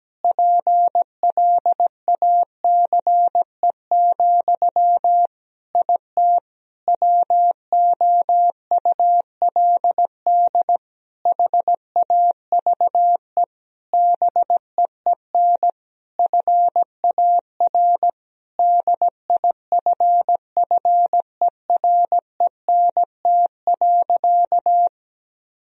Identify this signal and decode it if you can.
Morse code, heard as a keyed tone, about 17 wpm, 700 Hz. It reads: PLACE, IT WOULD HAVE BEEN FAR DIFFERENT.
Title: Sentences from Top 1000 Words - Review Part 3 - 17wpm